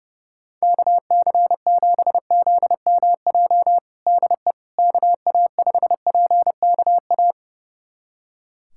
Morse code, heard as a keyed tone, about 30 wpm, 700 Hz. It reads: KC7ZMJ DE KA5PKA
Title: USA Callsign DE Callsign - 6 Characters - 30wpm